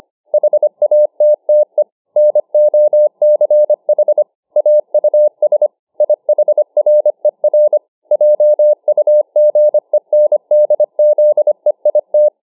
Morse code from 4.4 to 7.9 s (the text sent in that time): AUS IHRER